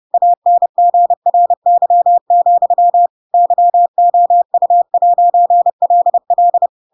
Morse code, heard as a keyed tone, about 30 wpm, 700 Hz. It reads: ANGRY, YOU'LL
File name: Pinocchio / Pinocchio ch4 30 30 ik1hge